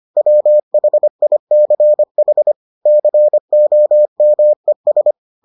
Morse code, heard as a keyed tone, about 25 wpm, 600 Hz. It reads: WHICH COMES